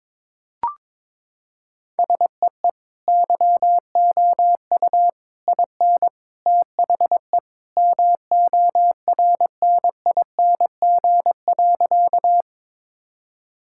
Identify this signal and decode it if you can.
Morse code, heard as a keyed tone, about 22 wpm, 700 Hz. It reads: SEE YOU IN THE MORNING.
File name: practice / 22wpm / Sentences with 3-Word Phrases - 22wpm